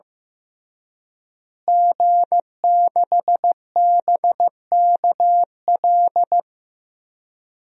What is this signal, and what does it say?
Morse code, heard as a keyed tone, about 15 wpm, 700 Hz. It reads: G6BKL
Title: DXCC-100 Call Signs - 5 Characters - Repeated 3x - 15wpm